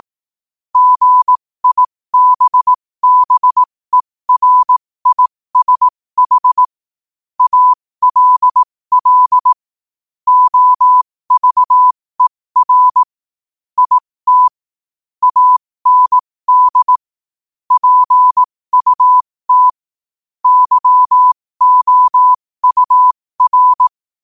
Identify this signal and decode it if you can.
Morse code, heard as a keyed tone, about 18 wpm, spaced as for 15 wpm, 1000 Hz. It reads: GIBBERISH ALL OVER IT AND PUT YOUR
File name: qotdfiles / 15wpm